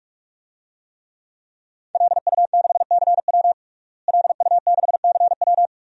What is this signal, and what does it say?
Morse code, heard as a keyed tone, about 46 wpm, 700 Hz. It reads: LU6CW LU6CW
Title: DXCC-100 Call Signs - 5 Characters - Repeated 2x - 45wpm